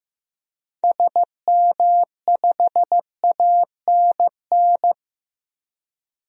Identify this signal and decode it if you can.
Morse code, heard as a keyed tone, about 15 wpm, 700 Hz. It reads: SM5ANN
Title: DXCC-100 Call Signs - 6 Characters - 15wpm